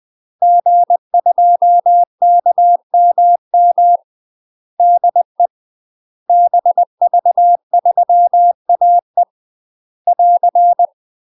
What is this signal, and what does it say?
Morse code, heard as a keyed tone, about 20 wpm, 700 Hz. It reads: G2KMM DE BV3AE <AR>